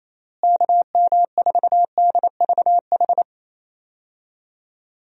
Morse code, heard as a keyed tone, about 28 wpm, 700 Hz. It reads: KM4DVH
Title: USA Call Signs - 6 Characters - Repeated 3x - 28wpm